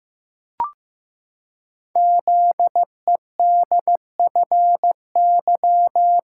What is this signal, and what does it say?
Morse code, heard as a keyed tone, about 15 wpm, 700 Hz. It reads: ZEDFY
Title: Code Groups - 5 Letters - 15wpm